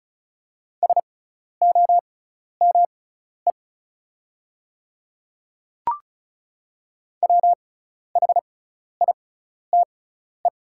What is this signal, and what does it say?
Morse code, heard as a keyed tone, about 36 wpm, spaced as for 12 wpm, 690 Hz. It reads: SOME WHITE
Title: 6x Character Spacing - Top 300 Words - 35wpm